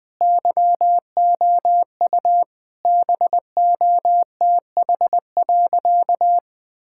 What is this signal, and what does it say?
Morse code, heard as a keyed tone, about 20 wpm, 700 Hz. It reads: YOU BOTH.
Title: Sentences from Top 1000 Words - Review Part 6 - 20wpm